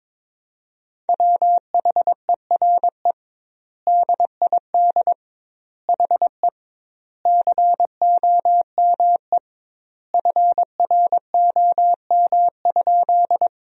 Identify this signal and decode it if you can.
Morse code, heard as a keyed tone, about 22 wpm, 700 Hz. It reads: WHERE DID HE COME FROM?
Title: 2x Word-Spacing - Sentences from Top 200 Words - 22wpm